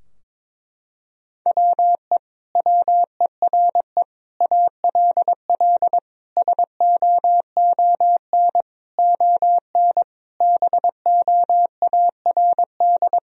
Morse code, heard as a keyed tone, about 22 wpm, 700 Hz. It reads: WE WERE ALL SOON ON BOARD
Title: Sentences from Top 1000 Words - Review Part 4 - 22wpm